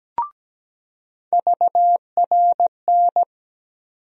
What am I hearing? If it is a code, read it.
Morse code, heard as a keyed tone, about 17 wpm, 700 Hz. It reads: VRN
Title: USA Call Sign Postfixes - 3 Letters 17wpm